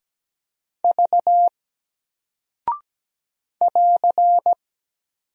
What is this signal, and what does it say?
Morse code, heard as a keyed tone, about 17 wpm, 700 Hz. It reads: V <AR>
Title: Single Character (Letter:Number:Punctuation:Prosign) 17wpm